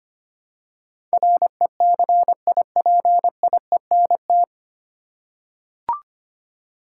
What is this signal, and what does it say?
Morse code, heard as a keyed tone, about 25 wpm, 700 Hz. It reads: RECIPIENT
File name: Nine Letter Words - 25wpm